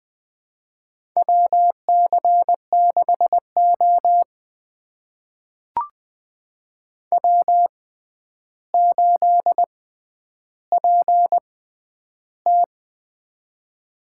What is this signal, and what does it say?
Morse code, heard as a keyed tone, about 20 wpm, spaced as for 10 wpm, 700 Hz. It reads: WC6O W8PT
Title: USA Call Signs - 4 Characters - 6x Character Spacing - 20wpm